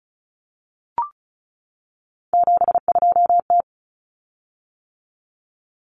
Morse code, heard as a keyed tone, about 34 wpm, 700 Hz. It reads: 72T
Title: Common Cut Numbers - 3-Digit 35wpm